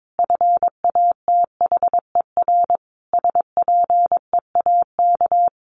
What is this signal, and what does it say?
Morse code, heard as a keyed tone, about 22 wpm, 700 Hz. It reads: FATHER SPEAK
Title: Sentences from Top 1100 Words 22wpm